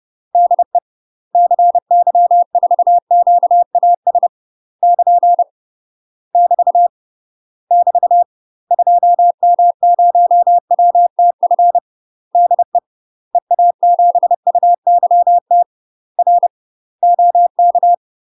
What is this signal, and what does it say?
Morse code, heard as a keyed tone, about 30 wpm, 700 Hz. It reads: DE CY4QAS <KN> <BT> <BT> 2M0WTF DE EA7UYT R OK
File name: UK MASTER QSOs TRAINING/30WPM